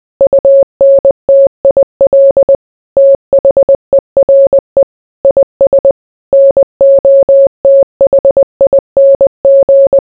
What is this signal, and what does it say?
Morse code, heard as a keyed tone, about 20 wpm, 550 Hz. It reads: UNTIL THERE IS NOTHING